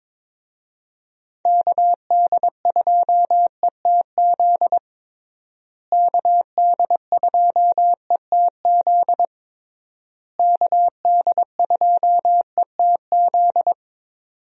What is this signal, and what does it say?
Morse code, heard as a keyed tone, about 22 wpm, 700 Hz. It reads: KD2ETZ KD2ETZ KD2ETZ
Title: USA Call Signs - 6 Characters - Repeated 3x - 22wpm